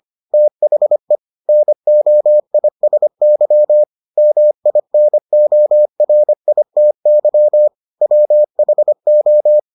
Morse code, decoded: THE NOISY MINORITY WHO

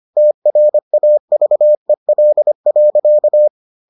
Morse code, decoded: TRAVEL.

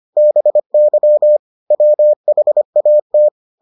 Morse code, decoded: BY WHAT

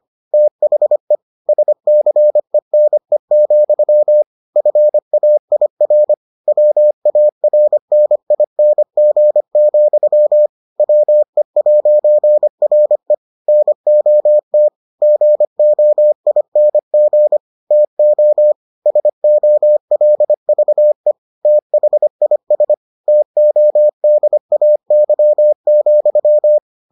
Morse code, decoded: THE SCENE, FAIR WARNING, WE'RE NOT GOING TO SOLVE THIS TODAY,

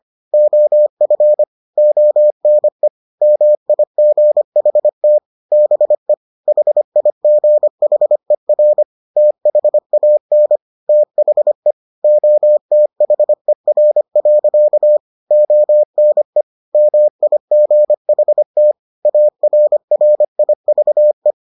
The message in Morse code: OF ONE MIGHT BE HIGHER THAN THE OTHER. ONE MIGHT ARRIVE

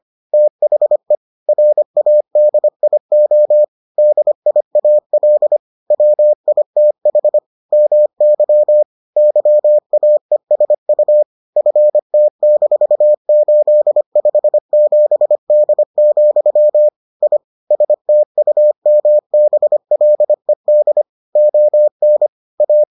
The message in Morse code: THE RADIO DIAL WITH MY YAESU FT-857D, I STUMBLED ON A

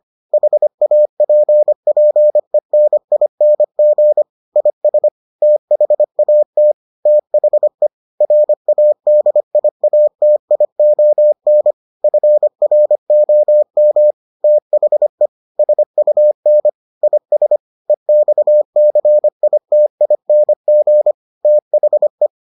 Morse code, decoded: HAPPENING IS THAT THE RADIATION FROM THE SUN IS EXCITING THE

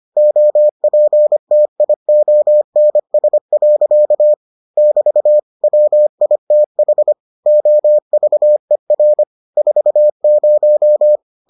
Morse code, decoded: OPTIONS. <BT> WITH OVER 40

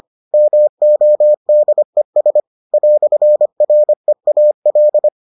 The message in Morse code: MODES "REAL

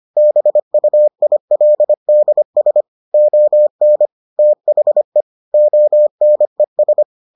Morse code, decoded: BUILDS ON THE ONES

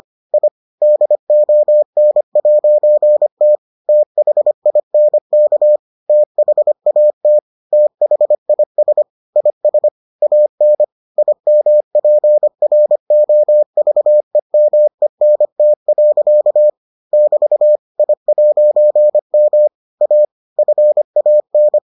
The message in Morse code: I DON'T THINK THAT THIS IS AN IMPROVEMENT. <BT> I'M A FAN

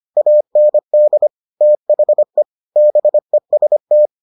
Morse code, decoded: AND THE BEST

